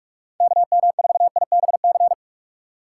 Morse code, heard as a keyed tone, about 47 wpm, 700 Hz. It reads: KM4IBC
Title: USA Call Signs - 2x Character Spacing - 45wpm